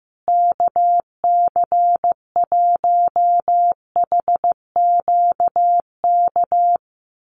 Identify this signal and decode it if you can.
Morse code, heard as a keyed tone, about 15 wpm, 700 Hz. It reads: KC1HQK
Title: USA Call Signs Repeated 3x 15wpm